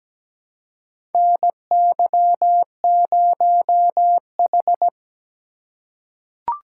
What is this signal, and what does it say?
Morse code, heard as a keyed tone, about 17 wpm, 700 Hz. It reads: NY0H
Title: USA Call Signs - 4 Characters - 2x Character Spacing - 17wpm